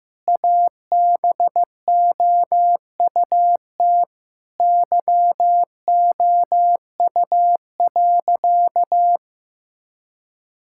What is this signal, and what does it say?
Morse code, heard as a keyed tone, about 15 wpm, 700 Hz. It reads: ABOUT YOU.
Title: Sentences from Top 1000 Words - Review Part 6 - 15wpm